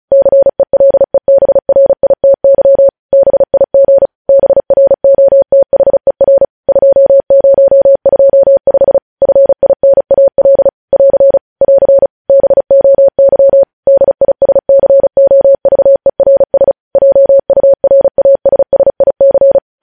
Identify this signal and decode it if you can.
Morse code, heard as a keyed tone, about 36 wpm, 550 Hz. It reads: CELEBRITY BIG BROTHER 2025 FINAL <AR> <AR> BOY DISCOVERS JURASSIC
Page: News headlines in Morse code at 35 wpm